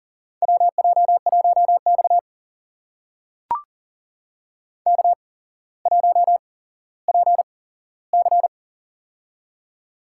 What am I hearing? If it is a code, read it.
Morse code, heard as a keyed tone, about 40 wpm, spaced as for 17 wpm, 700 Hz. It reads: WJ1X K1PC
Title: USA Call Signs - 4 Characters - 8x Character Spacing - 40wpm